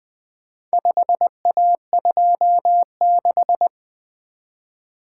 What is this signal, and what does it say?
Morse code, heard as a keyed tone, about 20 wpm, 700 Hz. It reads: 5A26
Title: Four Letter-Number Combinations - 20wpm